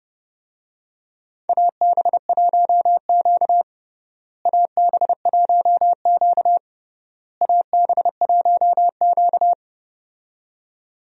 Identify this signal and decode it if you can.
Morse code, heard as a keyed tone, about 32 wpm, 700 Hz. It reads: AB1Q AB1Q AB1Q
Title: USA Call Signs - 4 Characters - Repeated 3x - 30wpm